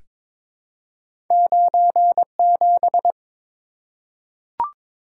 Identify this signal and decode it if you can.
Morse code, heard as a keyed tone, about 22 wpm, 700 Hz. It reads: 97
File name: practice / 22wpm / Two Letter-Number Combinations - 22wpm